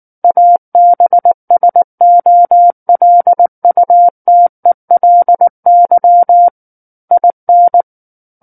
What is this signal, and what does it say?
Morse code, heard as a keyed tone, about 19 wpm, 700 Hz. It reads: ABSOLUTELY IN
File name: Book One / 10-Chapter 10 - 19 WPM